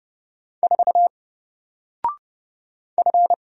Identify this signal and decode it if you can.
Morse code, heard as a keyed tone, about 30 wpm, 700 Hz. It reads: 4 F
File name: Single-Letter-Number-30wpm